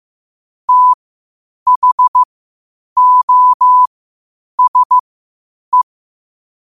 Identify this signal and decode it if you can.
Morse code, heard as a keyed tone, about 15 wpm, spaced as for 8 wpm, 1000 Hz. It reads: THOSE
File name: qotdfiles / 8wpm